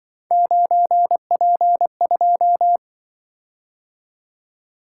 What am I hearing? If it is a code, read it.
Morse code, heard as a keyed tone, about 24 wpm, 700 Hz. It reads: 9P2
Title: Speed-Racing - Three Letter-Number Combinations - 20wpm